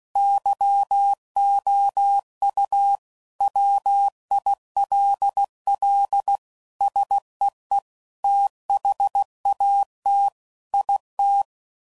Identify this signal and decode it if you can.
Morse code, heard as a keyed tone, about 16 wpm, 800 Hz. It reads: YOU WILL SEE THAT IT
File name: CW Code Groups @15WPM / Technical Subjects / OR Gate